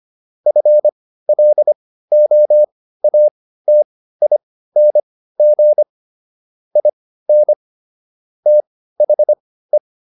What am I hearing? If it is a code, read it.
Morse code, heard as a keyed tone, about 25 wpm, spaced as for 15 wpm, 600 Hz. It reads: FLOATING IN THE
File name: SS101 15